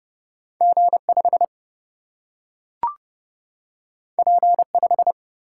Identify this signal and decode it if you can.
Morse code, heard as a keyed tone, about 31 wpm, 700 Hz. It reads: G 5 P 5
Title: Single-Letter Single-Number 30wpm